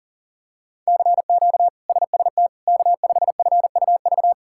Morse code, heard as a keyed tone, about 40 wpm, 700 Hz. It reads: CQ SST K5FUV
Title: K1USN SST QSO Practice - 40wpm